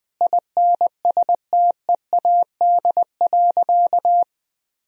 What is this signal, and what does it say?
Morse code, heard as a keyed tone, about 20 wpm, 700 Hz. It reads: INSTEAD.
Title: Sentences from Top 1000 Words - Review Part 4 - 20wpm